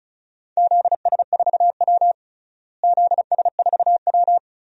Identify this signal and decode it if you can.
Morse code, heard as a keyed tone, about 34 wpm, 700 Hz. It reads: ZS4W ZS4W
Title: DXCC-100 Call Signs - 4 Characters - Repeated 3x - 35wpm